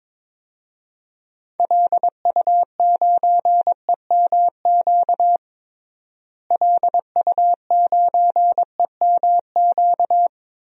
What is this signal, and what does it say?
Morse code, heard as a keyed tone, about 22 wpm, 700 Hz. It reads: LU9EMQ LU9EMQ
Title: DXCC-100 Call Signs - 6 Characters - Repeated 2x - 22wpm